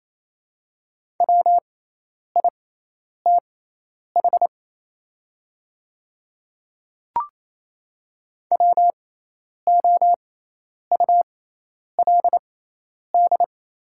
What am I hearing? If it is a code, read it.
Morse code, heard as a keyed tone, about 28 wpm, spaced as for 10 wpm, 690 Hz. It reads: WITH WOULD
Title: 6x Character Spacing - Top 300 Words - 28wpm